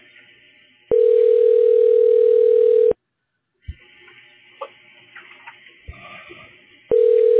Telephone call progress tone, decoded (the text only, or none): ringback tone